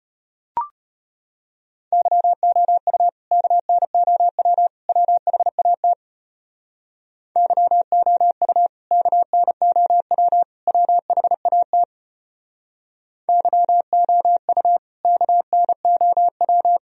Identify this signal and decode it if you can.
Morse code, heard as a keyed tone, about 33 wpm, 700 Hz. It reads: YOU KNOW WHAT YOU KNOW WHAT YOU KNOW